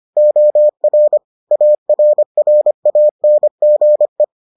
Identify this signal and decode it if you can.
Morse code, heard as a keyed tone, about 25 wpm, 600 Hz. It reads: OR ARRANGE